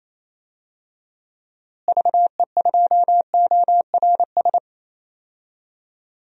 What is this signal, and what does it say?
Morse code, heard as a keyed tone, about 27 wpm, 700 Hz. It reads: VE2ORS
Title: DXCC-100 Call Signs - 6 Characters - Repeated 3x - 28wpm